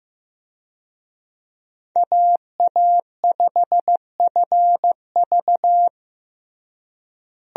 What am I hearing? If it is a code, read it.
Morse code, heard as a keyed tone, about 15 wpm, 700 Hz. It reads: AA5FV
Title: USA Call Signs - 5 Characters - Repeated 2x - 15wpm